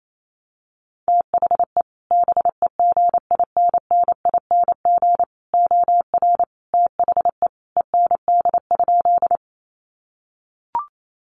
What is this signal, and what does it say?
Morse code, heard as a keyed tone, about 28 wpm, 700 Hz. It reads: THE BEGINNING OR THE END?
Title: Sentences from Top 800 Words 28wpm